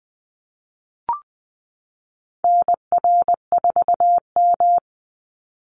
Morse code, heard as a keyed tone, about 20 wpm, 700 Hz. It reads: NR4M
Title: Call Signs Repeated 2x - 20wpm